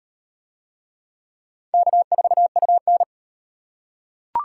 Morse code, heard as a keyed tone, about 36 wpm, 700 Hz. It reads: K4UN